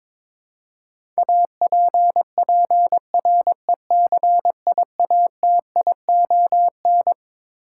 Morse code, heard as a keyed tone, about 22 wpm, 700 Hz. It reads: APPRECIATION
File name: Twelve Letter Words - 22wpm